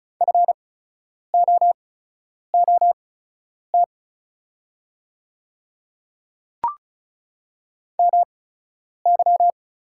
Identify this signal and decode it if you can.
Morse code, heard as a keyed tone, about 34 wpm, spaced as for 10 wpm, 700 Hz. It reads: FOOT MY
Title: 8x Character Spacing - Top 500 Words - 35wpm